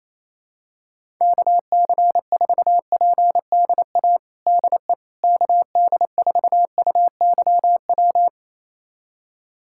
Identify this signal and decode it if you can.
Morse code, heard as a keyed tone, about 28 wpm, 700 Hz. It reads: KC4PDA DE KD4UYW
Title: USA Callsign DE Callsign - 6 Characters - 28wpm